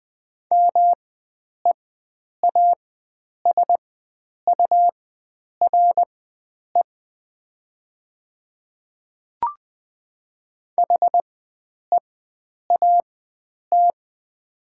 Morse code, heard as a keyed tone, about 20 wpm, spaced as for 9 wpm, 700 Hz. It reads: MEASURE HEAT
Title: 4x Character Spacing - Top 500 Words - 20wpm